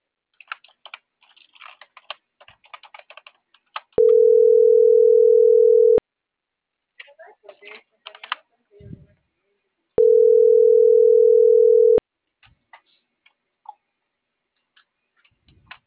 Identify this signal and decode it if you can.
Ringback tone (North America), 440 + 480 Hz.